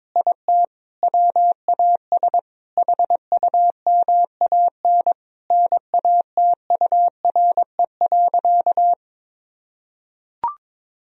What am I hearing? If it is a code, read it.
Morse code, heard as a keyed tone, about 22 wpm, 700 Hz. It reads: IT WAS HUMAN NATURE.
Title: Sentences from Top 1000 Words - Review Part 3 - 22wpm